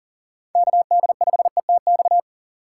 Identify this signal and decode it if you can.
Morse code, heard as a keyed tone, about 40 wpm, 700 Hz. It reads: KD5ETX
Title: USA Call Signs - 6 Characters - 2x Character Spacing - 40wpm